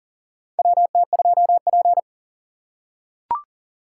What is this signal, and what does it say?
Morse code, heard as a keyed tone, about 40 wpm, 700 Hz. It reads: WT2P
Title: USA Call Signs - 4 Characters - 3x Character Spacing - 40wpm